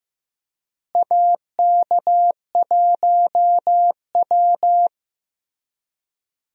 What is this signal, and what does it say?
Morse code, heard as a keyed tone, about 16 wpm, 700 Hz. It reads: AK1W
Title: USA Call Signs - 4 Characters - Repeated 3x - 15wpm